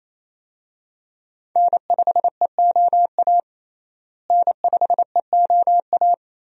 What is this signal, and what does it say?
Morse code, heard as a keyed tone, about 28 wpm, 700 Hz. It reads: N5EOA N5EOA
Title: USA Call Signs - 5 Characters - Repeated 2x - 28wpm